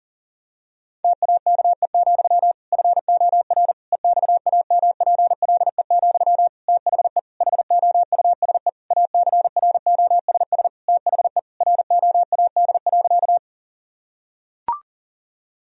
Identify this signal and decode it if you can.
Morse code, heard as a keyed tone, about 40 wpm, 700 Hz. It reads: TAKE, FOR EXAMPLE, THE HOUSE ACROSS THE ROAD.
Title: Sentences from Top 1000 Words - Review Part 1 - 40wpm